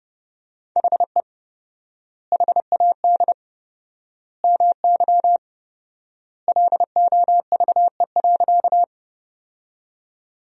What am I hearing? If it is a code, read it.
Morse code, heard as a keyed tone, about 30 wpm, 700 Hz. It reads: HE HAD MY LOVE.
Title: 4x Word-Spacing - Sentences from Top 400 Words - 30wpm